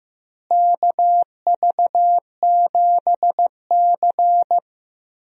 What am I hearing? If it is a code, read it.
Morse code, heard as a keyed tone, about 15 wpm, 700 Hz. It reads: KV7C
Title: USA Call Signs - 4 Characters - Repeated 3x - 15wpm